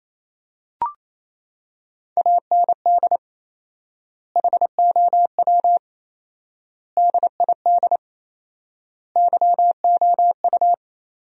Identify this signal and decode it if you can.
Morse code, heard as a keyed tone, about 27 wpm, 700 Hz. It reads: AND HOW DID YOU
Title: 4x Word-Spacing - Sentences from Top 200 Words - 28wpm